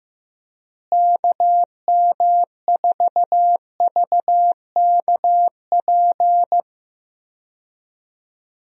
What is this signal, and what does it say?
Morse code, heard as a keyed tone, about 16 wpm, 700 Hz. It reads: KM4VKP
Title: USA Call Signs - 6 Characters - Repeated 2x - 15wpm